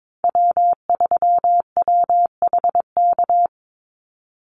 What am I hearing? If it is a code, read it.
Morse code, heard as a keyed tone, about 22 wpm, 700 Hz. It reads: W3WHK
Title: Call Signs - 22wpm